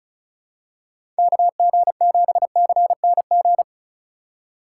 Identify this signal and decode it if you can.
Morse code, heard as a keyed tone, about 34 wpm, 700 Hz. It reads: KG7CNG
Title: USA Call Signs - 6 Characters - 4x Character Spacing - 35wpm